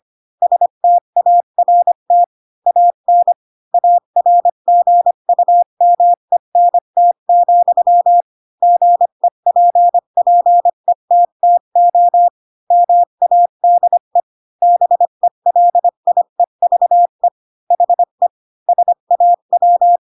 Morse code, heard as a keyed tone, about 25 wpm, 700 Hz. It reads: START AN ARGUMENT, GEPPETTO MADE BELIEVE HE SAW
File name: Pinocchio / Pinocchio ch3 20 25 ik1hge